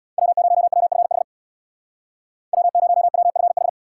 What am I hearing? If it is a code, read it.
Morse code, heard as a keyed tone, about 62 wpm, 700 Hz. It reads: W0WFH W0WFH
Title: Speed Racing - USA Call Signs 50wpm